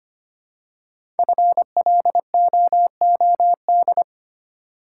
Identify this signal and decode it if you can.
Morse code, heard as a keyed tone, about 25 wpm, 700 Hz. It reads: FLOOD
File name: 5 Letter Words - First 3 Letters Given - 25wpm